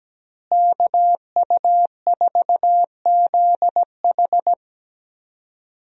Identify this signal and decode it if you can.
Morse code, heard as a keyed tone, about 17 wpm, 700 Hz. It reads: KU4ZH
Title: USA Call Signs - 5 Characters - Repeated 3x - 17wpm